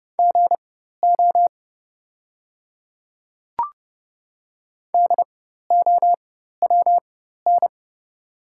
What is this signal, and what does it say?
Morse code, heard as a keyed tone, about 31 wpm, spaced as for 14 wpm, 690 Hz. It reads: GO DOWN
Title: 4x Character Spacing - Top 100 Words - 30wpm